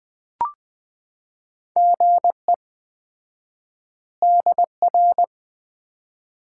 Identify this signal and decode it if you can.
Morse code, heard as a keyed tone, about 20 wpm, 700 Hz. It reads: GE DR
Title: Common QSO Phrases - 4x Word-Spacing - 20wpm